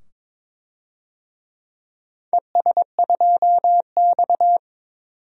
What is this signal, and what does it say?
Morse code, heard as a keyed tone, about 22 wpm, 700 Hz. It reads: ES2X
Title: DXCC-100 Call Signs - 4 Characters - 22wpm